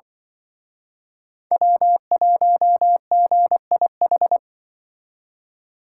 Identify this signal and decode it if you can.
Morse code, heard as a keyed tone, about 24 wpm, 700 Hz. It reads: W1GIH
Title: Speed Racing - USA 1x3 Call Signs 20wpm